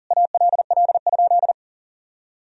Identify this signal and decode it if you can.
Morse code, heard as a keyed tone, about 40 wpm, 700 Hz. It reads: ALL?
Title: Sentences from Top 1000 Words - Review Part 2 - 40wpm